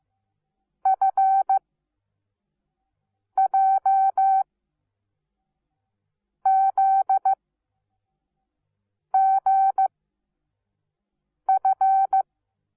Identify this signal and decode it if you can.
Morse code, heard as a keyed tone, about 15 wpm, spaced as for 4 wpm, 770 Hz. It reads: FJZGF
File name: Onlinelehrgang / morsen / DJ4UF Morsen17